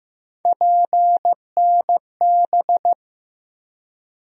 Morse code, heard as a keyed tone, about 15 wpm, 700 Hz. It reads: PNB
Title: USA Call Sign Postfixes - 3 Letters 15wpm